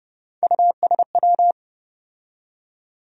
Morse code, heard as a keyed tone, about 30 wpm, 700 Hz. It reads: USW
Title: USA Call Sign Postfixes - 3 Letters 30wpm